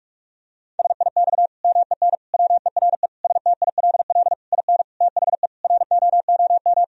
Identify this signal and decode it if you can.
Morse code, heard as a keyed tone, about 46 wpm, 700 Hz. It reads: SIX MEN WERE STILL IN THE ROOM
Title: Sentences from Top 1000 Words - Review Part 2 - 45wpm